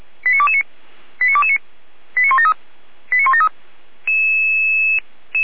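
Five-tone selective call (ZVEI), heard as quick runs of five tones, tones of about 70 ms. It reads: 89229 89229 89173 89173
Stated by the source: Five-tone ZVEI selective call received on an SDR, from feuerwehr-orlamuende.org (fire brigade Orlamünde).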